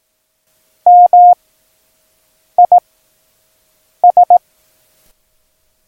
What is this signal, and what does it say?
Morse code, heard as a keyed tone, about 18 wpm, 710 Hz. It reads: M I S
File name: acs / Instrument Rating / Instrument ACS